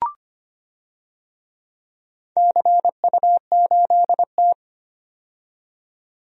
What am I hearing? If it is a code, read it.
Morse code, heard as a keyed tone, about 25 wpm, 700 Hz. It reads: CU8T
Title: DXCC-100 Call Signs - 4 Characters - 25wpm